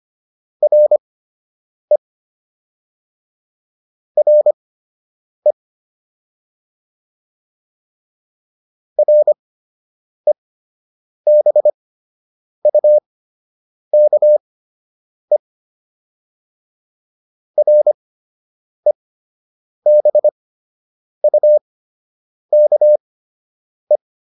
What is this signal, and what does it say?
Morse code, heard as a keyed tone, about 25 wpm, spaced as for 8 wpm, 600 Hz. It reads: RE RE REBUKE REBUKE